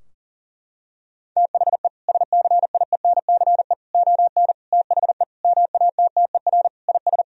Morse code, heard as a keyed tone, about 40 wpm, 700 Hz. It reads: THE SCIENCE ON THE MATTER IS
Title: Sentences from Top 1000 Words - Review Part 2 - 40wpm